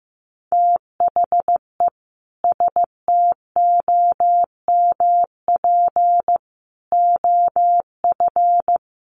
Morse code, heard as a keyed tone, about 15 wpm, 700 Hz. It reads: THE STOMP OF
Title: A Sound Of Thunder 15wpm